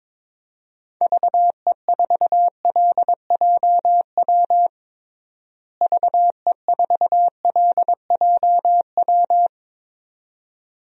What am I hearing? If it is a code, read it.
Morse code, heard as a keyed tone, about 22 wpm, 700 Hz. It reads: VE4LJW VE4LJW
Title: DXCC-100 Call Signs - 6 Characters - Repeated 2x - 22wpm